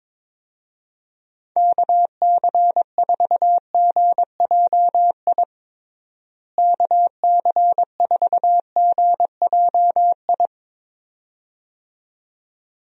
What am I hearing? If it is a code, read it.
Morse code, heard as a keyed tone, about 21 wpm, 700 Hz. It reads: KC4GJI KC4GJI
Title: USA Call Signs - 6 Characters - Repeated 2x - 22wpm